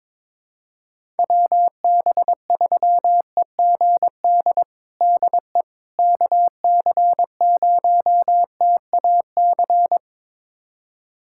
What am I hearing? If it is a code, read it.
Morse code, heard as a keyed tone, about 22 wpm, 700 Hz. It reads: WB3EGD DE KC0TAC
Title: USA Callsign DE Callsign - 6 Characters - 22wpm